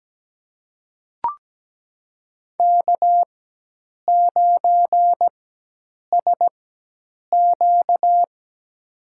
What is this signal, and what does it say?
Morse code, heard as a keyed tone, about 17 wpm, spaced as for 8 wpm, 700 Hz. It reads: K9SQ
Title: USA Call Signs - 4 Characters - 4x Character Spacing - 17wpm